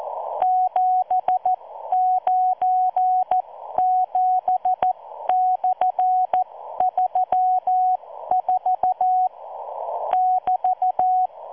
Morse code, heard as a keyed tone, about 14 wpm, spaced as for 12 wpm, 740 Hz. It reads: 797/34 <BT>